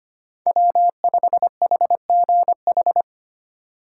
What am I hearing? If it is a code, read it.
Morse code, heard as a keyed tone, about 32 wpm, 700 Hz. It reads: W5HGH